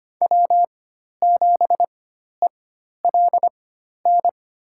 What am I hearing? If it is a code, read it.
Morse code, heard as a keyed tone, about 25 wpm, spaced as for 12 wpm, 700 Hz. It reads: W7ELN
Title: USA Call Signs - 5 Characters - 4x Character Spacing - 25wpm